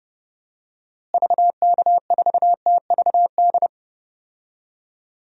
Morse code, heard as a keyed tone, about 30 wpm, 700 Hz. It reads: VK4TVD
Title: DXCC-100 Call Signs - 6 Characters - Repeated 2x - 30wpm